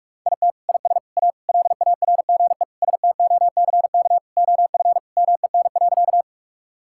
Morse code, heard as a keyed tone, about 44 wpm, 700 Hz. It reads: IT IS A LARGE STOCK OF MEN.
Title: Sentences from Top 1000 Words - Review Part 5 - 45wpm